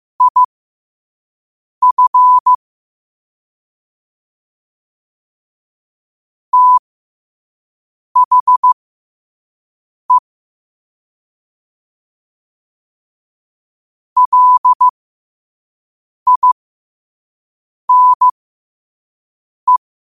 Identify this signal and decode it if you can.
Morse code, heard as a keyed tone, about 15 wpm, spaced as for 5 wpm, 1000 Hz. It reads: IF THE LINE
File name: 5wpm